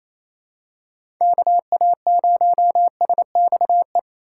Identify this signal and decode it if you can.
Morse code, heard as a keyed tone, about 28 wpm, 700 Hz. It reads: KA0SXE
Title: USA Call Signs - 6 Characters - 2x Character Spacing - 28wpm